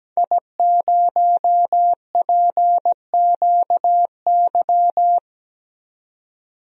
Morse code, heard as a keyed tone, about 17 wpm, 700 Hz. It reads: I0PQY
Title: DXCC-100 Call Signs - 5 Characters - 17wpm